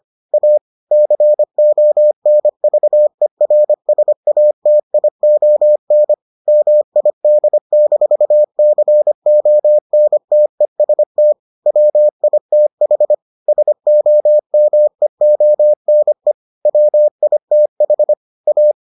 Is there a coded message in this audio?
Yes: Morse code from 0.3 to 18.7 s, heard as a keyed tone, about 25 wpm, 600 Hz. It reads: A CONVERSATION MID-CONTEST WITH SOMEONE WITH A